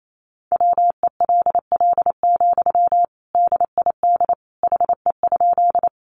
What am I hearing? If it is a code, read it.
Morse code, heard as a keyed tone, about 28 wpm, 700 Hz. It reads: WELL, DID HE?
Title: Sentences from Top 200 Words 28wpm